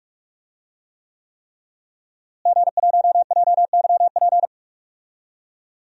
Morse code, heard as a keyed tone, about 46 wpm, 700 Hz. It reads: G1JYP
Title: DXCC-100 Call Signs - 5 Characters - Repeated 3x - 45wpm